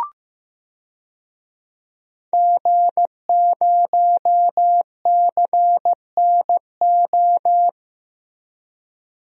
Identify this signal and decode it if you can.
Morse code, heard as a keyed tone, about 15 wpm, 700 Hz. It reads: G0CNO
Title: DXCC-100 Call Signs - 5 Characters - Repeated 3x - 15wpm